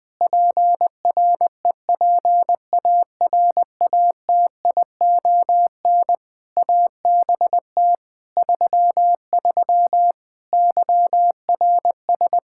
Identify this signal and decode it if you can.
Morse code, heard as a keyed tone, about 20 wpm, 700 Hz. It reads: PREPARATION ABT 33 YRS
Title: K1USN SST QSO Practice 20wpm